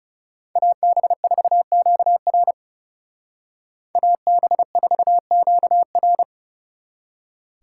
Morse code, heard as a keyed tone, about 38 wpm, 700 Hz. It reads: AB4QR AB4QR
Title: Speed Racing - USA Call Signs 30wpm